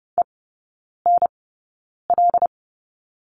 Morse code, heard as a keyed tone, about 30 wpm, spaced as for 9 wpm, 700 Hz. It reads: ENL
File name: Lesson 16e - Letter c - Three Characters 30wpm